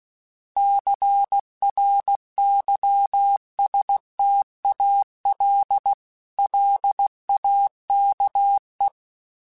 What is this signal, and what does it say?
Morse code, heard as a keyed tone, about 16 wpm, 800 Hz. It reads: CRYSTAL LAKE
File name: fromSMs / FromSMs mp3s / Cities@15